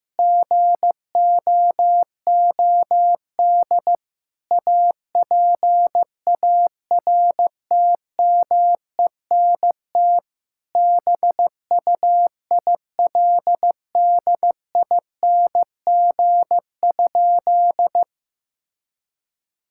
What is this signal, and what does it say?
Morse code, heard as a keyed tone, about 15 wpm, 700 Hz. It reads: GOOD APARTMENT BUILDING?
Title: Sentences from Top 1000 Words - Review Part 6 - 15wpm